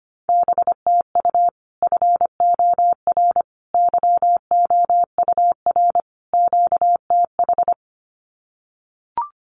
Morse code, heard as a keyed tone, about 25 wpm, 700 Hz. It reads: BTU FOR YOUR QTH
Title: Common QSO Phrases 25wpm